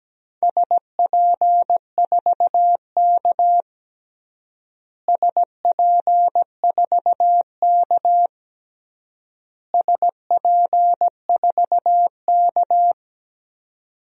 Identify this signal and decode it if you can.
Morse code, heard as a keyed tone, about 17 wpm, 700 Hz. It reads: SP4K SP4K SP4K
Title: DXCC-100 Call Signs - 4 Characters - Repeated 3x - 17wpm